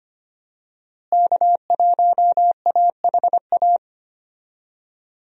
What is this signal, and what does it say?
Morse code, heard as a keyed tone, about 25 wpm, 700 Hz. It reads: K1AHA
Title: USA Call Signs - 5 Characters - Repeated 3x - 25wpm